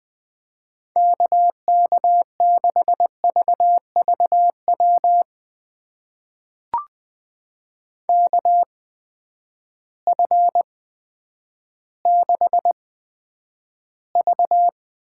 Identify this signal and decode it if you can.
Morse code, heard as a keyed tone, about 21 wpm, spaced as for 10 wpm, 700 Hz. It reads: KK6VVW KF6V
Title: USA Call Signs - 6 Characters - 8x Character Spacing - 20wpm